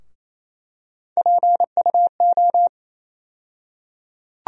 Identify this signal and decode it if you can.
Morse code, heard as a keyed tone, about 28 wpm, 700 Hz. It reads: PUO
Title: USA Call Sign Postfixes - 3 Letters 28wpm